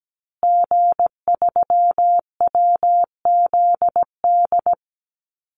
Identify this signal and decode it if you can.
Morse code, heard as a keyed tone, about 17 wpm, 700 Hz. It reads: G3WZD
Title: Call Signs Repeated 2x - 17wpm